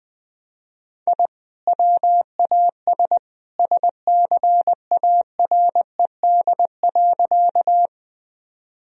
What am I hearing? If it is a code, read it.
Morse code, heard as a keyed tone, about 20 wpm, 700 Hz. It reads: I WAS SCARED.
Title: Sentences from Top 1000 Words - Review Part 4 - 20wpm